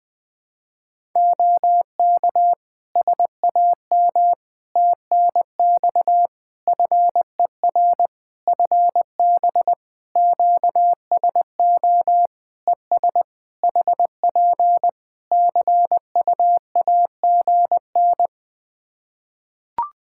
Morse code, heard as a keyed tone, about 20 wpm, 700 Hz. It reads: OK SAM TNX FER FB QSO ES HP CUAGN
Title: Practice QSO - Exchange 4 - Ending - 20wpm